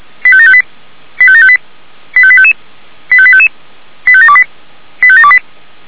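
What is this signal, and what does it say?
Five-tone selective call (ZVEI), heard as quick runs of five tones, tones of about 70 ms. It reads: 85758 85758 85855 85855 85728 85728
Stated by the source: ZVEI five-tone selective call received via SDR.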